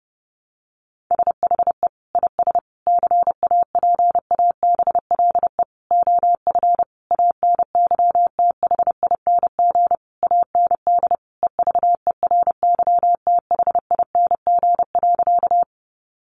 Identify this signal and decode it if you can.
Morse code, heard as a keyed tone, about 30 wpm, 700 Hz. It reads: SHE IS CAPABLE OF ANYTHING AND EVERYTHING.
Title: Sentences from Top 1200 Words 30wpm